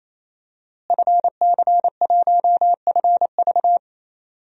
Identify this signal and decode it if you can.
Morse code, heard as a keyed tone, about 28 wpm, 700 Hz. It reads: FC1FV
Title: DXCC-100 Call Signs - 5 Characters - Repeated 3x - 28wpm